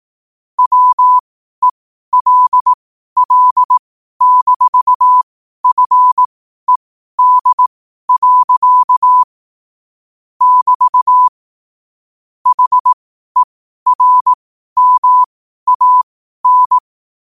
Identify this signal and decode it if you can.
Morse code, heard as a keyed tone, about 18 wpm, spaced as for 12 wpm, 1000 Hz. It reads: WELL-FED. <BT> HERMAN
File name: qotdfiles / 12wpm